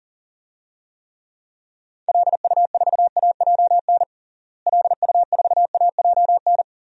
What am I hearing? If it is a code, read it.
Morse code, heard as a keyed tone, about 40 wpm, 700 Hz. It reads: LU4AJN LU4AJN
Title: DXCC-100 Call Signs - 6 Characters - Repeated 2x - 40wpm